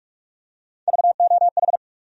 Morse code, heard as a keyed tone, about 47 wpm, 700 Hz. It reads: VOH
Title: USA Call Sign Postfixes - 3 Letters 45wpm